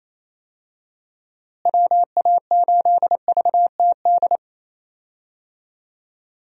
Morse code, heard as a keyed tone, about 28 wpm, 700 Hz. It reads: WA8VTD
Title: USA Call Signs - 6 Characters - Repeated 2x - 28wpm